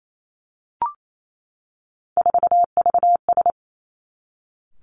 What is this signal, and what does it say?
Morse code, heard as a keyed tone, about 28 wpm, 700 Hz. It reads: 4VS
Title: Three Challenging Letter-Number Combinations 28wpm